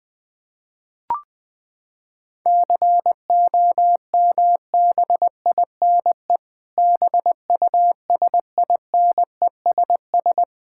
Morse code, heard as a keyed tone, about 21 wpm, 700 Hz. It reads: COMBINE BUSINESS
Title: Sentences from Top 1600 Words 20wpm